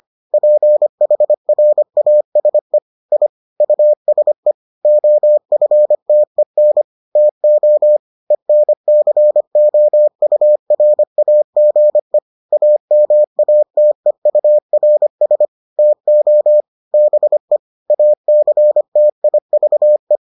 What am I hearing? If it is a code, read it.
Morse code, heard as a keyed tone, about 25 wpm, 600 Hz. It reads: PHRASE I USE OFTEN TO ENCOURAGE AMATEURS TO BE ACTIVE